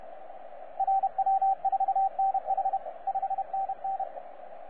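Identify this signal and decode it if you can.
Morse code, heard as a keyed tone, about 31 wpm, 780 Hz, with radio noise. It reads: RW4NH 5NN
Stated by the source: 9U4U on 15 metre CW, DX station recording by DX World.net.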